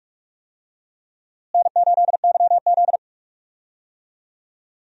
Morse code, heard as a keyed tone, about 46 wpm, 700 Hz. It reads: N8YZ
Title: USA Call Signs - 4 Characters - Repeated 3x - 45wpm